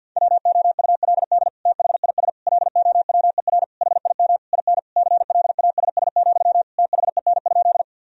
Morse code, heard as a keyed tone, about 50 wpm, 700 Hz. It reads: WOULD THIS LOWER HIM IN CLASS, THEN?
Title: Sentences from Top 1000 Words - Review Part 4 - 50wpm